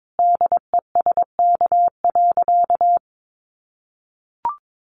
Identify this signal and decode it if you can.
Morse code, heard as a keyed tone, about 22 wpm, 700 Hz. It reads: DESK.
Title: Additional Easy Phrases 22wpm